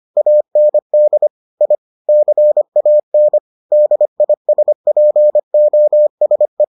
Morse code, heard as a keyed tone, about 25 wpm, 600 Hz. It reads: AND I CAN DISPOSE